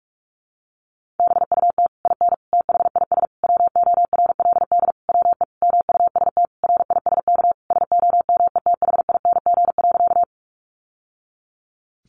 Morse code, heard as a keyed tone, about 44 wpm, 700 Hz. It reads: BUT IN THIS WORLD WE MUST RISK SOMETHING.
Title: Sentences from Top 600 Words 45wpm